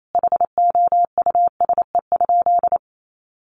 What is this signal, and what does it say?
Morse code, heard as a keyed tone, about 28 wpm, 700 Hz. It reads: HOUSE?
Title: Sentences from Top 200 Words 28wpm